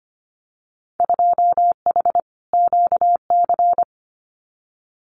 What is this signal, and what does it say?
Morse code, heard as a keyed tone, about 25 wpm, 700 Hz. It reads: 2H QC
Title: Winter Field Day Exchanges 25wpm